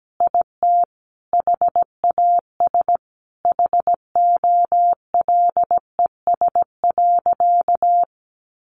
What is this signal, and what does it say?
Morse code, heard as a keyed tone, about 17 wpm, 700 Hz. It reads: IT HAS HOLES.
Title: Additional Easy Phrases 17wpm